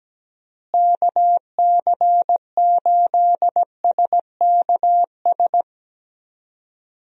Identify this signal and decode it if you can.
Morse code, heard as a keyed tone, about 17 wpm, 700 Hz. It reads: KC8SKS
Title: USA Call Signs - 6 Characters - Repeated 3x - 17wpm